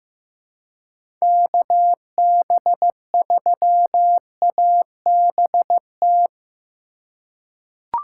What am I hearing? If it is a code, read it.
Morse code, heard as a keyed tone, about 15 wpm, 700 Hz. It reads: KB3ABT
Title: USA Call Signs - 6 Characters - 4x Character Spacing - 15wpm